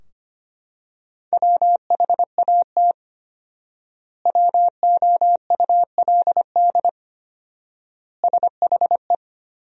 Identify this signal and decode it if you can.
Morse code, heard as a keyed tone, about 25 wpm, 700 Hz. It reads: WHAT WOULD SHE
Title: 4x Word-Spacing - Sentences from Top 100 Words - 25wpm